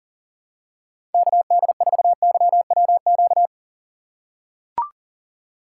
Morse code, heard as a keyed tone, about 40 wpm, 700 Hz. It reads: KD4YWQ
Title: USA Call Signs - 6 Characters - 6x Character Spacing - 40wpm